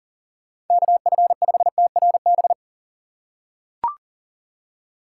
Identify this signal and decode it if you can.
Morse code, heard as a keyed tone, about 38 wpm, 700 Hz. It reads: KF5TRB
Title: USA Call Signs - 2x Character Spacing - 40wpm